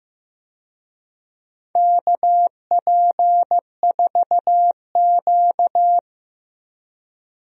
Morse code, heard as a keyed tone, about 15 wpm, 700 Hz. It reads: KP4Q
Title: DXCC-100 Call Signs - 4 Characters - 15wpm